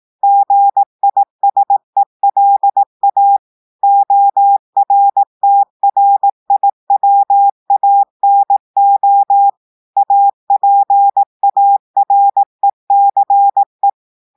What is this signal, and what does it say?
Morse code, heard as a keyed tone, about 18 wpm, 800 Hz. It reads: GISELA ORTRIWANO APARECE
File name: audios / portugais / portugais texte en clair 1 vitesse 18 mots minutes